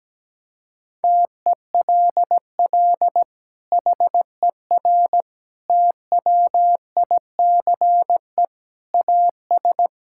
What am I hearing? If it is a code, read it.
Morse code, heard as a keyed tone, about 17 wpm, 700 Hz. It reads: TELL HER TWICE AS
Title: Sentences from Top 1600 Words 17wpm